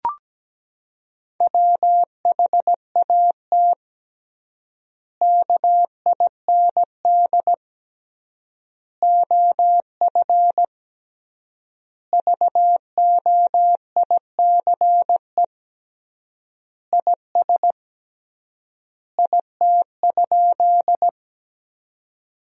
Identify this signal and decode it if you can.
Morse code, heard as a keyed tone, about 17 wpm, 700 Hz. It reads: WHAT KIND OF VOICE IS IT?
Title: 3x Word-Spacing - Sentences from Top 500 Words - 17wpm